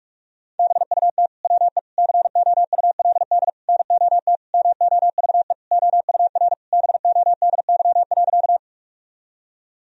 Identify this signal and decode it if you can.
Morse code, heard as a keyed tone, about 41 wpm, 700 Hz. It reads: BUT WE COULD NOT MOVE OUR BODY.
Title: Sentences from Top 1000 Words - Review Part 2 - 45wpm